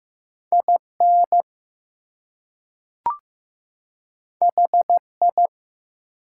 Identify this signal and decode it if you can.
Morse code, heard as a keyed tone, about 15 wpm, 700 Hz. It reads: IN HI
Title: Two Letter Words - 15wpm